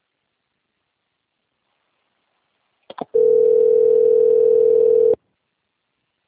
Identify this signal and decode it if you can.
Ringback tone (North America), 440 + 480 Hz.
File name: out- - - - - .20715